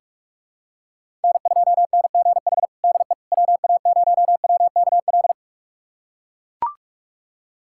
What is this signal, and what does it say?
Morse code, heard as a keyed tone, about 42 wpm, 700 Hz. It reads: N2NGH DE WA0WKL
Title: USA Callsign DE Callsign 45wpm